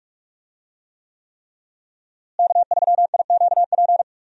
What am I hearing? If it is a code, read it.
Morse code, heard as a keyed tone, about 44 wpm, 700 Hz. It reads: K3IQP
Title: USA Call Signs - 5 Characters - Repeated 3x - 45wpm